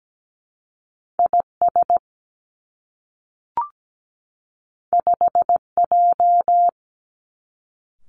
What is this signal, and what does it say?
Morse code, heard as a keyed tone, about 17 wpm, 700 Hz. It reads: IS 5J
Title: Two Challenging Letter-Number Combinations 17wpm